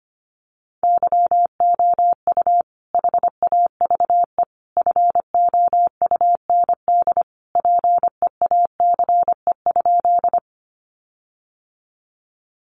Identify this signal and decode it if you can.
Morse code, heard as a keyed tone, about 25 wpm, 700 Hz. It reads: YOU HAVE FOUND PEACE?
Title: Sentences from Top 1500 Words 25wpm